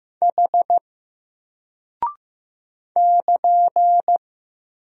Morse code, heard as a keyed tone, about 15 wpm, 700 Hz. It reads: H <KN>
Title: Single Character (Letter:Number:Punctuation:Prosign) Alt 15wpm